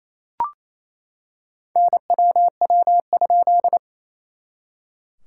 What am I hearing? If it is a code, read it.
Morse code, heard as a keyed tone, about 28 wpm, 690 Hz. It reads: NWW?
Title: SOTA QSO Practice 28wpm